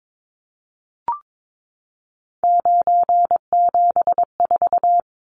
Morse code, heard as a keyed tone, about 22 wpm, 700 Hz. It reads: 974
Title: Numbers - 3-Digit 22wpm